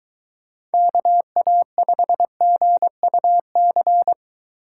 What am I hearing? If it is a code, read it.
Morse code, heard as a keyed tone, about 23 wpm, 700 Hz. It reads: KA5GUC